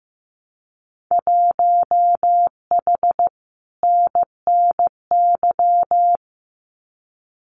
Morse code, heard as a keyed tone, about 15 wpm, 700 Hz. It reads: 1H NNY
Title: Winter Field Day Exchanges 15wpm